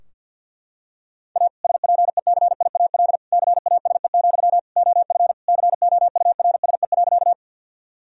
Morse code, heard as a keyed tone, about 52 wpm, 700 Hz. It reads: A SPECIAL CASE, OF COURSE.
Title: Sentences from Top 1000 Words - Review Part 3 - 50wpm